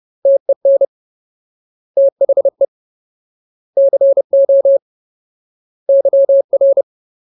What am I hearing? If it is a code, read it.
Morse code, heard as a keyed tone, about 30 wpm, 550 Hz. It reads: TEN THE CO YR
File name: Ultimate Word List / 30WPM Ultimate